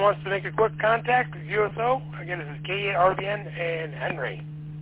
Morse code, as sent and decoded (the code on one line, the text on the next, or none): none